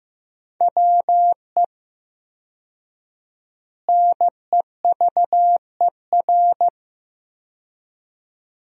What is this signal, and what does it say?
Morse code, heard as a keyed tone, about 15 wpm, 700 Hz. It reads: WE NEVER
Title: 4x Word-Spacing - Sentences from Top 300 Words - 15wpm